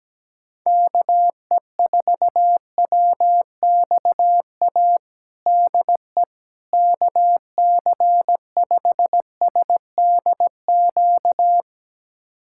Morse code, heard as a keyed tone, about 17 wpm, 700 Hz. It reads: KE4WXA DE KC5SDQ
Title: USA Callsign DE Callsign - 6 Characters - 17wpm